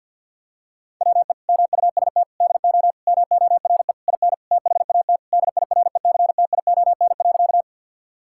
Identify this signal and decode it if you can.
Morse code, heard as a keyed tone, about 50 wpm, 700 Hz. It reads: WE MUST DO MORE IN THAT DIRECTION.
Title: Sentences from Top 1000 Words - Review Part 5 - 50wpm